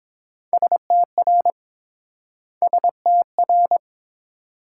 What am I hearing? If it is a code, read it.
Morse code, heard as a keyed tone, about 30 wpm, 700 Hz. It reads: STR STR